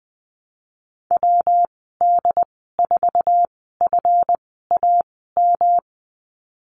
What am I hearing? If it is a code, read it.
Morse code, heard as a keyed tone, about 20 wpm, spaced as for 15 wpm, 700 Hz. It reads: WD4FAM
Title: 179 Lesson 29g - Slash - Call Signs 20wpm